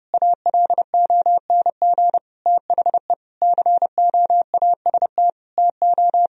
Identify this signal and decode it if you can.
Morse code, heard as a keyed tone, about 31 wpm, 700 Hz. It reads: ALONG THE COAST TO
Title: Sentences from Top 1000 Words - Review Part 4 - 30wpm